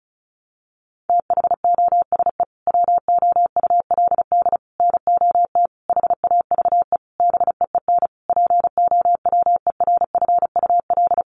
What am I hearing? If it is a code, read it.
Morse code, heard as a keyed tone, about 34 wpm, 700 Hz. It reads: THOSE WOULD NOT HAVE BEEN POWERFUL
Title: Sentences from Top 1300 Words 35wpm